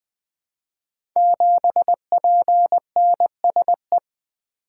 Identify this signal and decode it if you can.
Morse code, heard as a keyed tone, about 20 wpm, 700 Hz. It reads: 7PNSE
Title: Code Groups - 5 Letter:Numbers - 20wpm